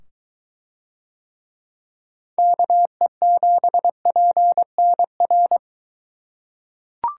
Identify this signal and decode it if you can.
Morse code, heard as a keyed tone, about 23 wpm, 700 Hz. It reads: KE7PNR